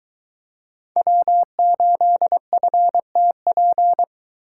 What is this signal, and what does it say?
Morse code, heard as a keyed tone, about 21 wpm, 700 Hz. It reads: W8FTP